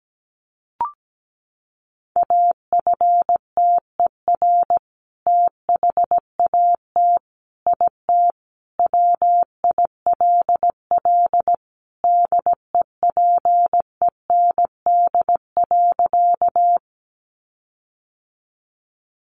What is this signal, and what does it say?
Morse code, heard as a keyed tone, about 17 wpm, 700 Hz. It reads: AFTER THAT IT WILL DEPEND.
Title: Sentences from Top 1300 Words 17wpm